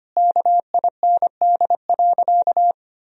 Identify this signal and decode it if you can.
Morse code, heard as a keyed tone, about 25 wpm, 700 Hz. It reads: KIND.
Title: Sentences from Top 500 Words and Abbreviations 25wpm